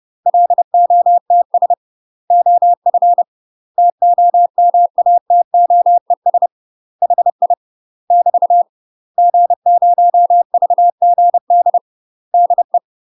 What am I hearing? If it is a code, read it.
Morse code, heard as a keyed tone, about 30 wpm, 700 Hz. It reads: LOTS OF TOMATOES HI <BT> G0VGD DE
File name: UK MASTER QSOs TRAINING / 30WPM / UK MASTER QSOs 0023